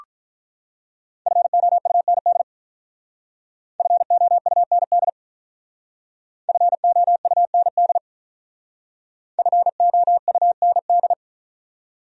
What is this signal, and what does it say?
Morse code, heard as a keyed tone, about 44 wpm, 700 Hz. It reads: FOUND FOUND FOUND FOUND